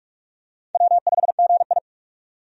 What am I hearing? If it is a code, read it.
Morse code, heard as a keyed tone, about 46 wpm, 700 Hz. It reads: W5GI
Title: USA Call Signs - 4 Characters - 45wpm